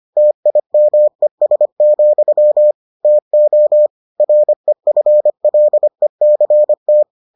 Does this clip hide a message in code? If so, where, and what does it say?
Yes: Morse code from 0.2 to 7.0 s, heard as a keyed tone, about 25 wpm, 600 Hz. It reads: TIMES, TO REFLECT